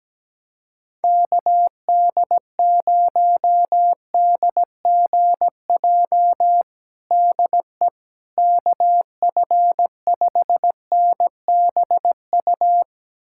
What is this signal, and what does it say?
Morse code, heard as a keyed tone, about 17 wpm, 700 Hz. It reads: KD0DGJ DE KF5NBU
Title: USA Callsign DE Callsign - 6 Characters - 17wpm